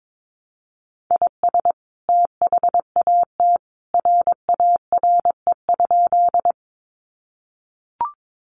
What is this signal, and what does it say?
Morse code, heard as a keyed tone, about 22 wpm, 700 Hz. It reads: IS THAT RARE?
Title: Sentences from Top 1100 Words 22wpm